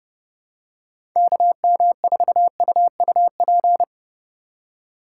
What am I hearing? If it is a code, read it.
Morse code, heard as a keyed tone, about 30 wpm, 700 Hz. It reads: KM4UUP
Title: USA Call Signs - 4x Character Spacing - 30wpm